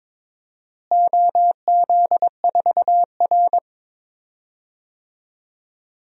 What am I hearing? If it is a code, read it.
Morse code, heard as a keyed tone, about 22 wpm, 700 Hz. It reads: OZ4R
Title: DXCC-100 Call Signs - 4 Characters - Repeated 3x - 22wpm